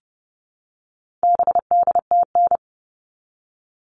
Morse code, heard as a keyed tone, about 29 wpm, 700 Hz. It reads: BDTN
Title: All Cut Numbers - 4-Digit 30wpm